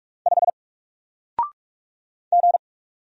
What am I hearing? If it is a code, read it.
Morse code, heard as a keyed tone, about 45 wpm, 700 Hz. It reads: 5 G
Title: Single Letter-Number Alt 45wpm